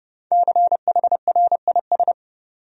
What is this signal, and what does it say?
Morse code, heard as a keyed tone, about 30 wpm, 700 Hz. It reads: CHRIS
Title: USA Common Names - Max 5 Letters - 30wpm